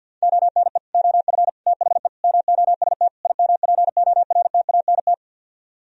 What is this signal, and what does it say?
Morse code, heard as a keyed tone, about 64 wpm, 700 Hz. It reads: ONE OF THE MOST IMPORTANT